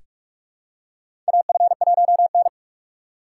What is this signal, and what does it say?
Morse code, heard as a keyed tone, about 46 wpm, 700 Hz. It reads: AF1N
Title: USA Call Signs - 4 Characters - 2x Character Spacing - 45wpm